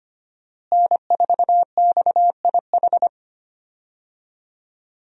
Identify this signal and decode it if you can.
Morse code, heard as a keyed tone, about 25 wpm, 700 Hz. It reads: N4XIH
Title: USA Call Signs - 5 Characters - Repeated 2x - 25wpm